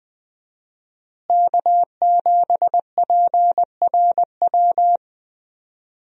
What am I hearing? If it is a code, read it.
Morse code, heard as a keyed tone, about 20 wpm, 700 Hz. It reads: K7PRW
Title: USA Call Signs - 5 Characters - Repeated 2x - 20wpm